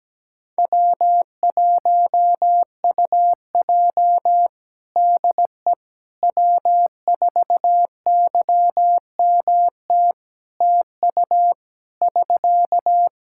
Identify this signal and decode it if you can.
Morse code, heard as a keyed tone, about 17 wpm, 700 Hz. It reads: W1UJ DE W4YMT TU <SK>
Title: Practice QSO - Exchange 4 - Ending - 17wpm